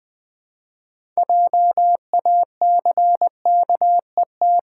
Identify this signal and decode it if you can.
Morse code, heard as a keyed tone, about 21 wpm, 700 Hz. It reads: JACKET
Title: Six Letter Words - 20wpm